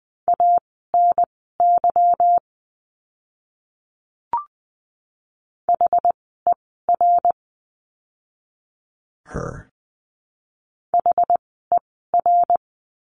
Morse code, heard as a keyed tone, about 20 wpm, spaced as for 15 wpm, 700 Hz. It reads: ANY HER HER
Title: 244 Lesson 39f - Letter z - Words 20wpm